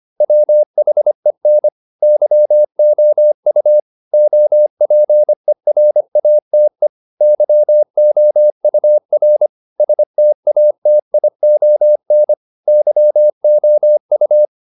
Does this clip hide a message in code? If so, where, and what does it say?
Yes: Morse code from 0.2 to 14.4 s, heard as a keyed tone, about 25 wpm, 600 Hz. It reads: WHEN YOU OPERATE YOUR STATION YOU